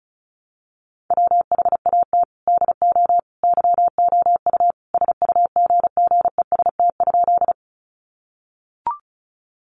Morse code, heard as a keyed tone, about 34 wpm, 700 Hz. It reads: WHAT DO YOU SUGGEST?
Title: Sentences from Top 1300 Words 35wpm